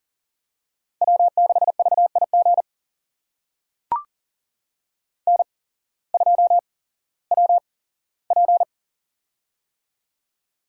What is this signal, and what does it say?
Morse code, heard as a keyed tone, about 42 wpm, spaced as for 19 wpm, 700 Hz. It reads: W6VIG N2WP
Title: USA Call Signs - 8x Character Spacing - 40wpm